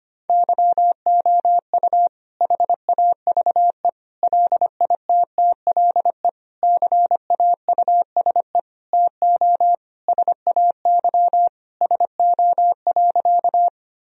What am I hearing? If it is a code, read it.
Morse code, heard as a keyed tone, about 25 wpm, 700 Hz. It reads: YOU HAVE LITTLE CAUSE TO SAY SO.
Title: Sentences from Top 1000 Words - Review Part 1 - 25wpm